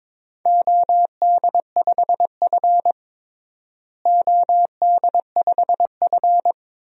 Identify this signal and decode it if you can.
Morse code, heard as a keyed tone, about 22 wpm, 700 Hz. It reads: OD5F OD5F
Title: DXCC-100 Call Signs - 4 Characters - Repeated 3x - 22wpm